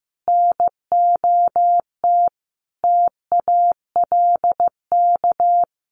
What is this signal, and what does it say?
Morse code, heard as a keyed tone, about 15 wpm, 700 Hz. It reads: NOT TALK
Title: Sentences from Top 1200 Words 15wpm